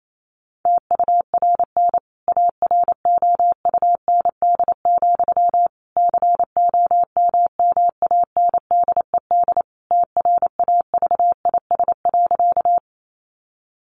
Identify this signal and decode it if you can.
Morse code, heard as a keyed tone, about 28 wpm, 700 Hz. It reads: TURN AROUND, COMMANDED TRAVIS.
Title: A Sound Of Thunder 28wpm